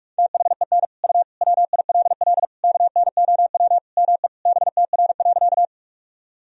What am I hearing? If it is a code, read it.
Morse code, heard as a keyed tone, about 41 wpm, 700 Hz. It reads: THEN U WILL KNOW ME BTR.
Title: Sentences from Top 400 Words and Abbreviations 45wpm